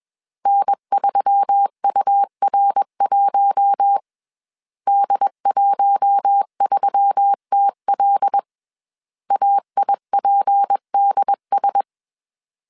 Morse code, heard as a keyed tone, about 21 wpm, 800 Hz. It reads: N3UR1 D13TL AIPDS
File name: CW / DET / 1 6 1500C S B 20wpm 2048